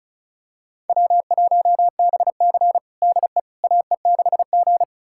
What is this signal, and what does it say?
Morse code, heard as a keyed tone, about 36 wpm, 700 Hz. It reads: W1BC DE AE6G
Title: USA Callsign DE Callsign - 4 Characters - 35wpm